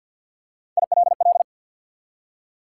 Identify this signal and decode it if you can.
Morse code, heard as a keyed tone, about 50 wpm, 700 Hz. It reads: ILL
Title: Three Letter Words - 50wpm